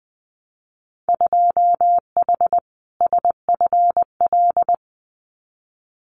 Morse code, heard as a keyed tone, about 20 wpm, 700 Hz. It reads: 2H SFL
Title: Winter Field Day Exchanges Encore 20wpm